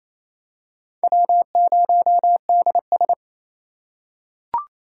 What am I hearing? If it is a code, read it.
Morse code, heard as a keyed tone, about 28 wpm, 700 Hz. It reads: W0DS
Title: USA Call Signs - 2x Character Spacing - 28wpm